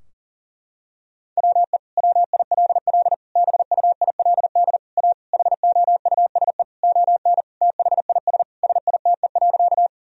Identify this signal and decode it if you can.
Morse code, heard as a keyed tone, about 37 wpm, 700 Hz. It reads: WE WILL BUILD A HOUSE ON THIS SITE.
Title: Sentences from Top 1000 Words - Review Part 4 - 40wpm